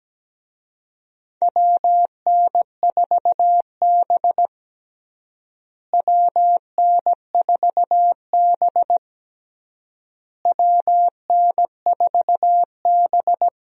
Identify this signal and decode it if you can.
Morse code, heard as a keyed tone, about 17 wpm, 700 Hz. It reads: WN4B WN4B WN4B
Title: USA Call Signs - 4 Characters - Repeated 3x - 17wpm